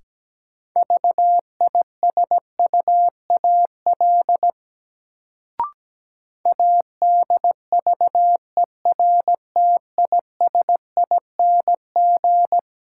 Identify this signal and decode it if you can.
Morse code, heard as a keyed tone, about 17 wpm, 700 Hz. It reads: VISUAL ADVERTISING
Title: Sentences from Top 1000 Words - Review Part 6 - New Word Drill - 17wpm